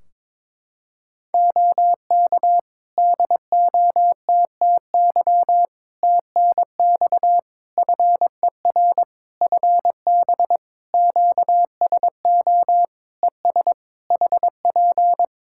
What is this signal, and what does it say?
Morse code, heard as a keyed tone, about 21 wpm, 700 Hz. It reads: OK DOTTY TNX FER FB QSO ES HP
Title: K1USN SST QSO Practice 22wpm